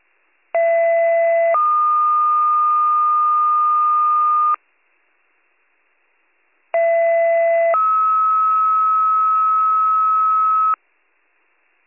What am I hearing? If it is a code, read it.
Two-tone page, heard as a one-second tone, then a three-second tone, about 670 then 1220 Hz, about 670 then 1290 Hz. It is heard from 0.5 to 10.7 s.